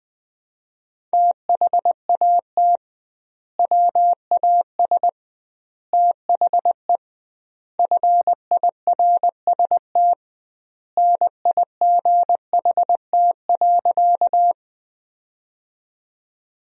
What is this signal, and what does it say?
Morse code, heard as a keyed tone, about 19 wpm, 700 Hz. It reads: THAT WAS THE FIRST NIGHT.
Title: 2x Word-Spacing - Sentences from Top 300 Words - 20wpm